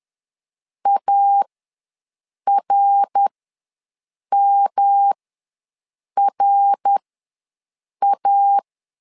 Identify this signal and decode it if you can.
Morse code, heard as a keyed tone, about 11 wpm, spaced as for 6 wpm, 800 Hz. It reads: ARMRA
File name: LEZIONI / 3 MWRO / 3 MWR ws 10wpm 2151